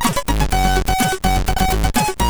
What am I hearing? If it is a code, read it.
There is no code.